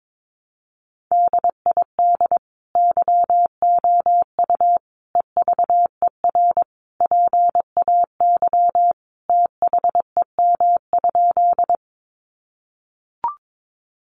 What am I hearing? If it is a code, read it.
Morse code, heard as a keyed tone, about 22 wpm, 700 Hz. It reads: DID YOU EVER PAY THEM?
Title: Sentences from Top 700 Words 22wpm